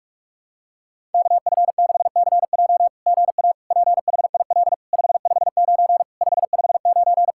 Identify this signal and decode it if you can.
Morse code, heard as a keyed tone, about 46 wpm, 690 Hz. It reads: KF6CJ GA PHIL 559 559
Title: SOTA QSO Practice 45wpm